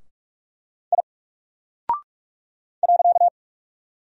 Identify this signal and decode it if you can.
Morse code, heard as a keyed tone, about 44 wpm, 700 Hz. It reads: I .
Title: Single Character (Letter:Number:Punctuation:Prosign) Alt 45wpm